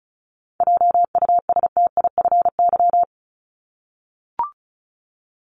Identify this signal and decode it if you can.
Morse code, heard as a keyed tone, about 34 wpm, 700 Hz. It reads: JUSTIFY
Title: Top 1751-2000 Words 35wpm